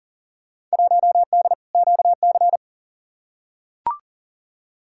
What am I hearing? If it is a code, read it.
Morse code, heard as a keyed tone, about 40 wpm, 700 Hz. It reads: 1D QC
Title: Field Day Exchanges Encore - 40wpm